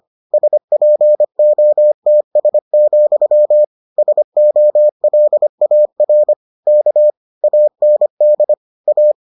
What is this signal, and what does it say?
Morse code, heard as a keyed tone, about 25 wpm, 600 Hz. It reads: SPOTS, SOLAR K AND A